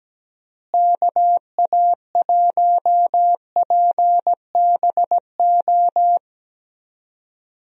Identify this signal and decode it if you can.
Morse code, heard as a keyed tone, about 17 wpm, 700 Hz. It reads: KA1PBO
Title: USA Call Signs - 6 Characters - Repeated 2x - 17wpm